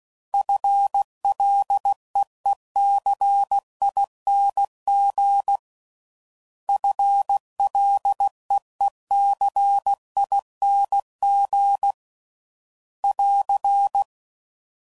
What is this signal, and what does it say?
Morse code, heard as a keyed tone, about 16 wpm, 800 Hz. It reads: FLEECING FLEECING <AR>